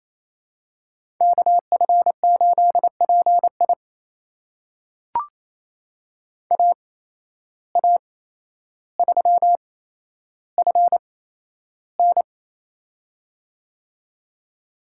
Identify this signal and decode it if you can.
Morse code, heard as a keyed tone, about 28 wpm, spaced as for 12 wpm, 700 Hz. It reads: KF8PI AA3FN
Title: USA Call Signs - 5 Characters - 8x Character Spacing - 28wpm